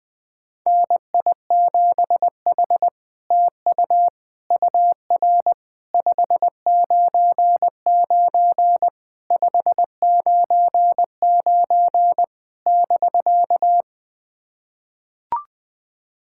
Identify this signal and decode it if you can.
Morse code, heard as a keyed tone, about 20 wpm, 700 Hz. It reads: NI7H TU UR 599 599 <BK>
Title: SOTA QSO Practice 20wpm